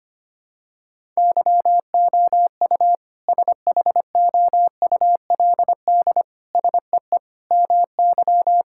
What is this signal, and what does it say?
Morse code, heard as a keyed tone, about 25 wpm, 700 Hz. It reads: YOU SHOULD SEE MY
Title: Sentences from Top 1600 Words 25wpm